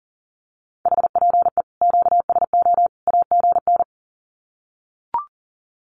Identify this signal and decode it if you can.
Morse code, heard as a keyed tone, about 40 wpm, 700 Hz. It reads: HPE QSO AGN
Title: Common QSO Phrases 40wpm